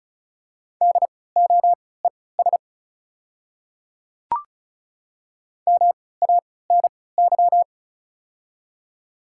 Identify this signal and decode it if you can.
Morse code, heard as a keyed tone, about 36 wpm, spaced as for 20 wpm, 690 Hz. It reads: DOES MANY
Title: 3x Character Spacing - Top 200 Words - 35wpm